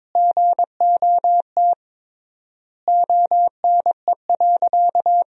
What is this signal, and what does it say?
Morse code, heard as a keyed tone, about 21 wpm, 700 Hz. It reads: GOT ONE.
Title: 3x Word-Spacing - Sentences from Top 300 Words - 22wpm